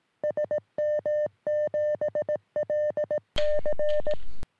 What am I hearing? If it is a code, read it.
Morse code, heard as a keyed tone, about 17 wpm, 600 Hz. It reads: SM7LC
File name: sm7lcb / ok2poi 3cm